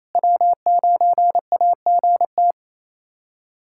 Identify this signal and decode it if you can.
Morse code, heard as a keyed tone, about 28 wpm, 700 Hz. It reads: W9AGT
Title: USA Callsign DE Callsign - 5 Characters - 28wpm